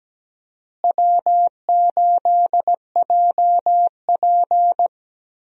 Morse code, heard as a keyed tone, about 17 wpm, 700 Hz. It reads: W8JP
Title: USA Call Signs - 4 Characters - Repeated 3x - 17wpm